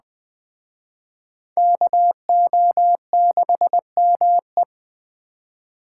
Morse code, heard as a keyed tone, about 21 wpm, 700 Hz. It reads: KO6ME